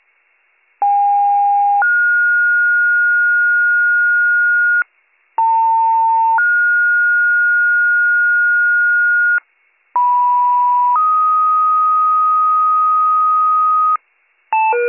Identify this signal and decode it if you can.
Two-tone page, heard as a one-second tone, then a three-second tone, about 810 then 1430 Hz, about 900 then 1430 Hz, about 980 then 1250 Hz, with noise. It is heard from 0.8 to 14.0 s.